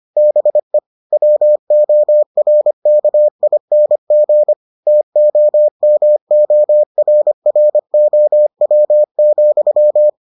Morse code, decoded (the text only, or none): BE WORKING TOMORROW,